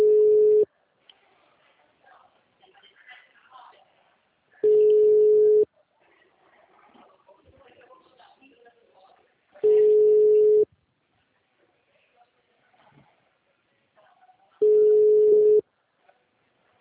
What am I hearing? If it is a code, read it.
Ringback tone (Europe, 425 Hz), with speech.